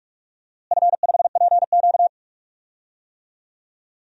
Morse code, heard as a keyed tone, about 46 wpm, 700 Hz. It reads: F5PQ
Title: DXCC-100 Call Signs - 4 Characters - Repeated 3x - 45wpm